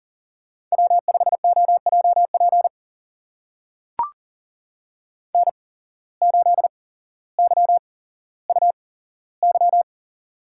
Morse code, heard as a keyed tone, about 41 wpm, spaced as for 17 wpm, 700 Hz. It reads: W5OJP N8YUY
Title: USA Call Signs - 8x Character Spacing - 40wpm